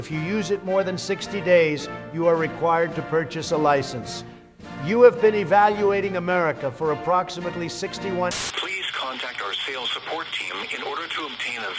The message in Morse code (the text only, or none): none